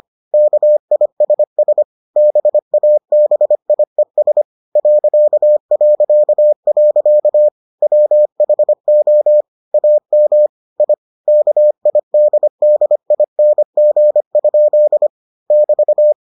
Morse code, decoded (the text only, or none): KISS BABIES ... WHO AM I KIDDING? <BT>